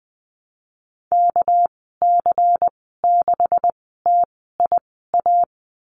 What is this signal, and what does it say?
Morse code, heard as a keyed tone, about 20 wpm, spaced as for 15 wpm, 700 Hz. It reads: KC6TIA
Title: 231 Lesson 37g - Letter x - Call Signs 20wpm